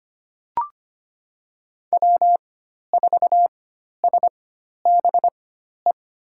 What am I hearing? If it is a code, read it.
Morse code, heard as a keyed tone, about 25 wpm, spaced as for 12 wpm, 700 Hz. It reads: W4SBE
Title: USA Call Signs - 5 Characters - 4x Character Spacing - 25wpm